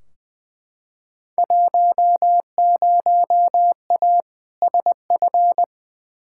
Morse code, heard as a keyed tone, about 21 wpm, 700 Hz. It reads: 10A SF
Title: Field Day Exchanges Encore - 20wpm